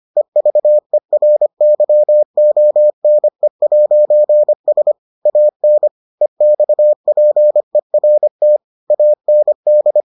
Morse code, heard as a keyed tone, about 25 wpm, 600 Hz. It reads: EVERYONE'S AN EXPERT AND